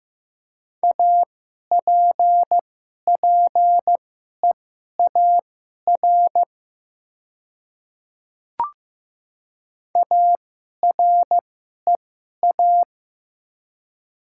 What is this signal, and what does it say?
Morse code, heard as a keyed tone, about 15 wpm, spaced as for 11 wpm, 700 Hz. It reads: APPEAR AREA
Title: 2x Character Spacing - Top 500 Words - 15wpm